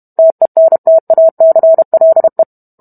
Morse code, heard as a keyed tone, about 32 wpm, 650 Hz. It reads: TENTACLE